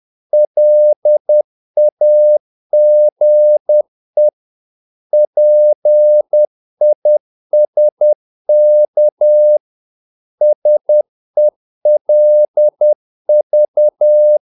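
Morse code, heard as a keyed tone, about 10 wpm, 600 Hz. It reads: LAGE PISK SELV
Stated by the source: Morse practice of short Danish words at 10 wpm, from CW med Gnister.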